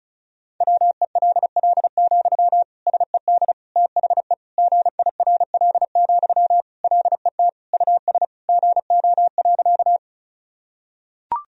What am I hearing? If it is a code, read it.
Morse code, heard as a keyed tone, about 36 wpm, 700 Hz. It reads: WELL, SED THE GIRL, LET US GO.
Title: Sentences from Top 400 Words and Abbreviations 35wpm